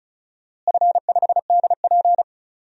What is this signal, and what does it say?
Morse code, heard as a keyed tone, about 36 wpm, 700 Hz. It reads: F5DP
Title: DXCC-100 Call Signs - 4 Characters - Repeated 2x - 35wpm